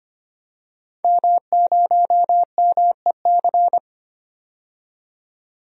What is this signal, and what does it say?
Morse code, heard as a keyed tone, about 25 wpm, 700 Hz. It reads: M0MEC
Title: DXCC-100 Call Signs - 5 Characters - 25wpm